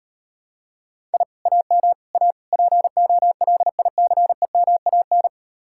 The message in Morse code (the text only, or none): I AM A POLICEMAN